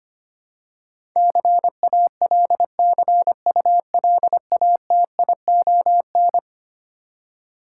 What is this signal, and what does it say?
Morse code, heard as a keyed tone, about 25 wpm, 700 Hz. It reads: CALCULATION
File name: practice / 25wpm / Eleven Letter Words - 25wpm